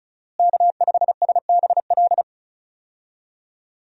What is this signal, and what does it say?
Morse code, heard as a keyed tone, about 36 wpm, 700 Hz. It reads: K5SBL
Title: USA Call Signs - 5 Characters - Repeated 2x - 35wpm